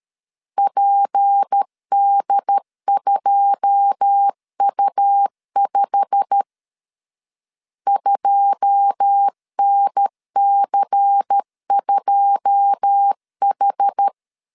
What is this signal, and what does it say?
Morse code, heard as a keyed tone, about 13 wpm, 800 Hz. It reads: PD2U5 2NC2H
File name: LEZIONI / 9 2 12wpm 2301